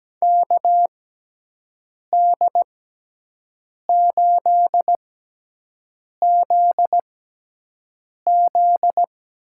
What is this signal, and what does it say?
Morse code, heard as a keyed tone, about 17 wpm, spaced as for 6 wpm, 700 Hz. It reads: KD8ZZ
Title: USA Call Signs - 6 Characters - 6x Character Spacing - 17wpm